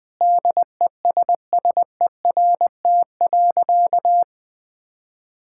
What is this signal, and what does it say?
Morse code, heard as a keyed tone, about 21 wpm, 700 Hz. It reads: DESSERT.
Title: Sentences with 5-Word Phrases - 20wpm